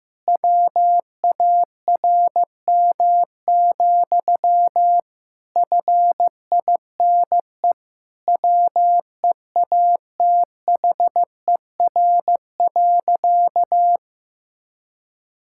Morse code, heard as a keyed tone, about 16 wpm, 700 Hz. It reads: WARM, FINE WEATHER.
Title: Sentences from Top 1000 Words - Review Part 5 - 15wpm